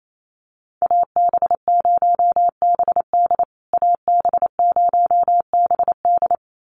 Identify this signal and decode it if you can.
Morse code, heard as a keyed tone, about 28 wpm, 700 Hz. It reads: AB0BD AB0BD
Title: USA Call Signs Repeated 2x 28wpm